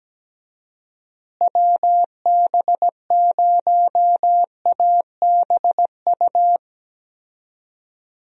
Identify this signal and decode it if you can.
Morse code, heard as a keyed tone, about 17 wpm, 700 Hz. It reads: WB0ABU
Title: USA Call Signs - 6 Characters - Repeated 3x - 17wpm